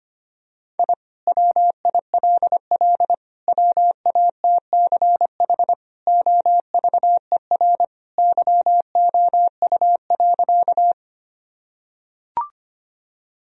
Morse code, heard as a keyed tone, about 25 wpm, 700 Hz. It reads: I WILL WATCH OVER YOU.
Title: Sentences from Top 1000 Words - Review Part 1 - 25wpm